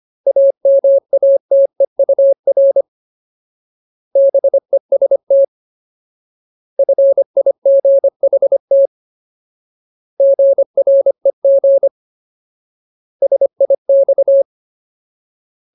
Morse code, heard as a keyed tone, about 25 wpm, 550 Hz. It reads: AMATEUR BEST FIGHT GREG SIX
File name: Ultimate Word List / 25WPM Ultimate